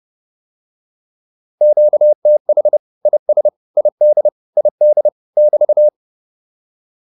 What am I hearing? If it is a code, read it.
Morse code, heard as a keyed tone, about 30 wpm, 600 Hz. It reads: QTH IS ID ID <BT>